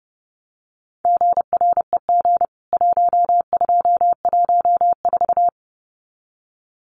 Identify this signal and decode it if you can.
Morse code, heard as a keyed tone, about 29 wpm, 700 Hz. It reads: GREG 1214
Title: CWT Exchange 30wpm